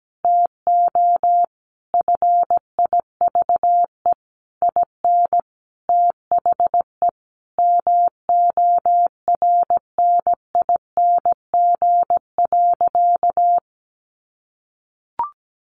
Morse code, heard as a keyed tone, about 17 wpm, 700 Hz. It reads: TO FIVE IN THE MORNING.
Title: Sentences from Top 1300 Words 17wpm